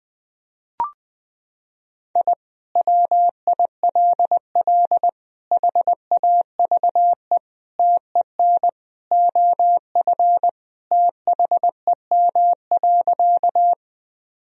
Morse code, heard as a keyed tone, about 20 wpm, 700 Hz. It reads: I WILL HAVE TEN OF THEM.
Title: Sentences from Top 1000 Words - Review Part 2 - 20wpm